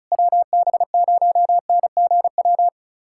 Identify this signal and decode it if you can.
Morse code, heard as a keyed tone, about 36 wpm, 700 Hz. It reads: WB0NGW
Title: USA Callsign DE Callsign - 6 Characters - 35wpm